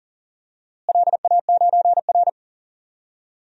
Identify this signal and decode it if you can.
Morse code, heard as a keyed tone, about 37 wpm, 700 Hz. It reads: LA9R
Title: DXCC-100 Call Signs - 4 Characters - Repeated 2x - 40wpm